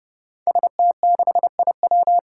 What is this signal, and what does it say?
Morse code, heard as a keyed tone, about 29 wpm, 700 Hz. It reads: ST6IW
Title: Five Challenging Letter-Number Combinations 30wpm